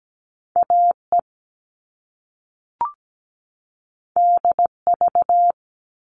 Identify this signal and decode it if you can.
Morse code, heard as a keyed tone, about 17 wpm, 700 Hz. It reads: AE DV
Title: All Cut Numbers - 2-Digit 17wpm